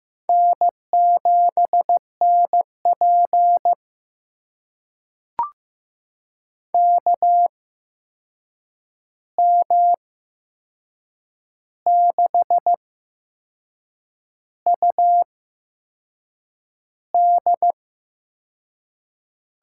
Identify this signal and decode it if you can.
Morse code, heard as a keyed tone, about 15 wpm, spaced as for 6 wpm, 700 Hz. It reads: N7NP KM6UD
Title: USA Call Signs - 8x Character Spacing - 15wpm